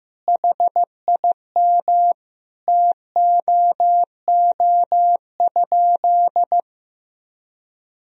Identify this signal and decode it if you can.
Morse code, heard as a keyed tone, about 16 wpm, 700 Hz. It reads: HIM TOO?
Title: Sentences from Top 500 Words and Abbreviations 15wpm